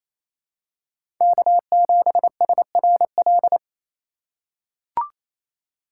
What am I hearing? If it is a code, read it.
Morse code, heard as a keyed tone, about 27 wpm, 700 Hz. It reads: K7SRL
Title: USA Call Signs - 5 Characters - 4x Character Spacing - 28wpm